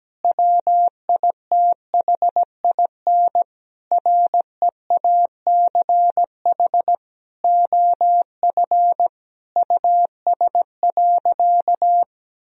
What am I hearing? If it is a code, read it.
Morse code, heard as a keyed tone, about 17 wpm, 700 Hz. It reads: WITHIN REACH OF US.
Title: Sentences from Top 1000 Words - Review Part 4 - 17wpm